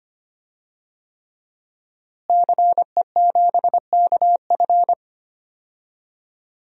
Morse code, heard as a keyed tone, about 25 wpm, 700 Hz. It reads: CE7KF
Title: DXCC-100 Call Signs - 5 Characters - Repeated 3x - 25wpm